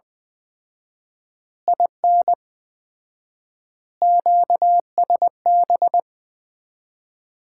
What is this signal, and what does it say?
Morse code, heard as a keyed tone, about 20 wpm, 700 Hz. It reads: IN QSB
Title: Common QSO Phrases - 4x Word-Spacing - 20wpm